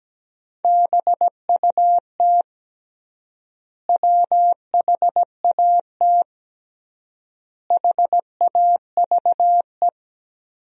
Morse code, heard as a keyed tone, about 17 wpm, 700 Hz. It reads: BUT WHAT HAVE
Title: 3x Word-Spacing - Sentences from Top 100 Words - 17wpm